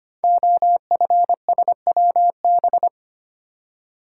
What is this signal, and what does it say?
Morse code, heard as a keyed tone, about 25 wpm, 700 Hz. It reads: OFSWB
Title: Code Groups - 5 Letters - 25wpm